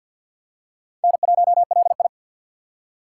Morse code, heard as a keyed tone, about 47 wpm, 700 Hz. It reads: N1LI
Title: USA Call Signs - 4 Characters - 6x Character Spacing - 50wpm